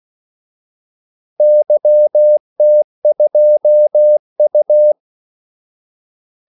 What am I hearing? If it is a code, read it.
Morse code, heard as a keyed tone, about 16 wpm, 600 Hz. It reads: YT2U
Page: Kaldesignaler 16 wpm